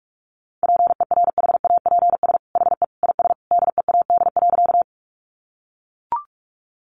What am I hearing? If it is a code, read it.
Morse code, heard as a keyed tone, about 44 wpm, 700 Hz. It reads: PERHAPS HE IS DEAD.
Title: Sentences from Top 1500 Words 45wpm